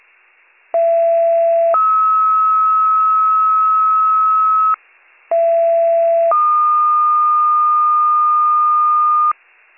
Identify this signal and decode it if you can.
Two-tone page, heard as a one-second tone, then a three-second tone, about 670 then 1290 Hz, about 670 then 1220 Hz. It is heard from 0.7 to 9.3 s.